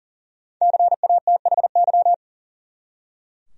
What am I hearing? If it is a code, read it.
Morse code, heard as a keyed tone, about 38 wpm, 700 Hz. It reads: CATHY
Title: USA Common Names - 40wpm